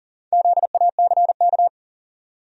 Morse code, heard as a keyed tone, about 40 wpm, 700 Hz. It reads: ZACK
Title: USA Common Names - Max 4 Letters - 40wpm